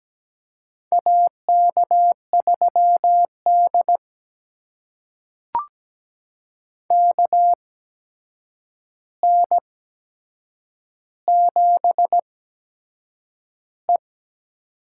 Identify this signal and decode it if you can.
Morse code, heard as a keyed tone, about 17 wpm, spaced as for 7 wpm, 700 Hz. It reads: AK3D KN7E
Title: USA Call Signs - 4 Characters - 8x Character Spacing - 17wpm